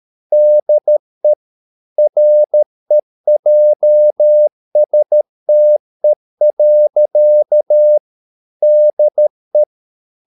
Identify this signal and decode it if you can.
Morse code, heard as a keyed tone, about 13 wpm, 600 Hz. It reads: DE REJSTE. DE
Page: Ferie 13 wpm